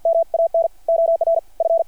Morse code, heard as a keyed tone, about 53 wpm, 660 Hz. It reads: MAN OR F
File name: archived / CRSnet